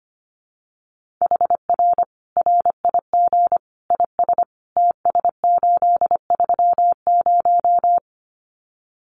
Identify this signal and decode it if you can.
Morse code, heard as a keyed tone, about 25 wpm, 700 Hz. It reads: HR RIG IS TS830
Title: Common QSO Phrases 25wpm